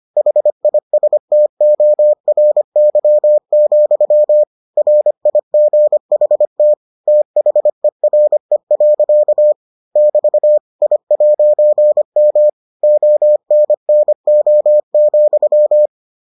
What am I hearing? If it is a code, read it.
Morse code, heard as a keyed tone, about 25 wpm, 600 Hz. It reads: HISTORY, RIGHT THERE. <BT> I'M ONNO,